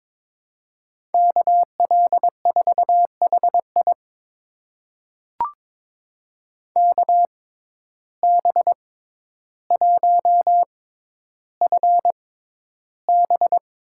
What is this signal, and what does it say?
Morse code, heard as a keyed tone, about 22 wpm, spaced as for 11 wpm, 700 Hz. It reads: KL4HI KB1FB
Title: USA Call Signs - 5 Characters - 6x Character Spacing - 22wpm